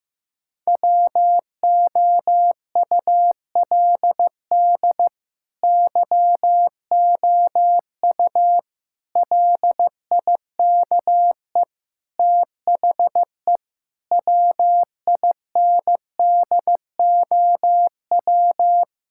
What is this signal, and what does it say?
Morse code, heard as a keyed tone, about 15 wpm, 700 Hz. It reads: WOULD YOU LIKE THE WINDOW
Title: Sentences from Top 1600 Words 15wpm